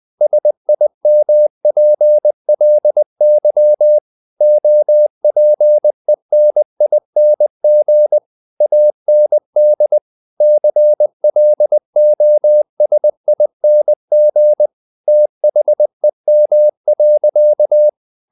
Morse code, heard as a keyed tone, about 20 wpm, 600 Hz. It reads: SIMPLY OPENING AND CLOSING THEM.
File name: the old man and the sea part 2 20wpm